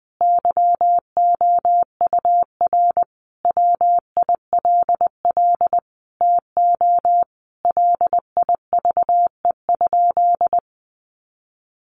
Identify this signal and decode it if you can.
Morse code, heard as a keyed tone, about 20 wpm, 700 Hz. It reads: YOUR WILL TO LIVE?
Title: Sentences from Top 1500 Words 20wpm